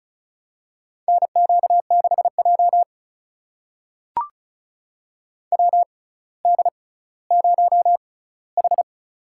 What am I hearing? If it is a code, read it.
Morse code, heard as a keyed tone, about 34 wpm, spaced as for 18 wpm, 700 Hz. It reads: NQ6J WD0H
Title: USA Call Signs - 4 Characters - 6x Character Spacing - 35wpm